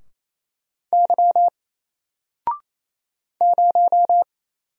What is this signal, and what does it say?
Morse code, heard as a keyed tone, about 27 wpm, 700 Hz. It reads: Y 0
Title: Single Character (Letter:Number:Punctuation:Prosign) 28wpm